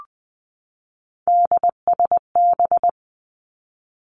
Morse code, heard as a keyed tone, about 20 wpm, 700 Hz. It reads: DSB
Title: Three Challenging Letter-Number Combinations 20wpm